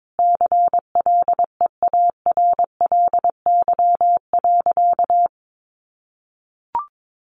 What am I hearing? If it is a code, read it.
Morse code, heard as a keyed tone, about 22 wpm, 700 Hz. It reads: CLEARLY.
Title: Sentences from Top 1300 Words 22wpm